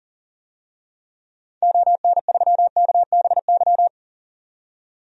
Morse code, heard as a keyed tone, about 40 wpm, 700 Hz. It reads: ON3KBY
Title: DXCC-100 Call Signs - 6 Characters - 40wpm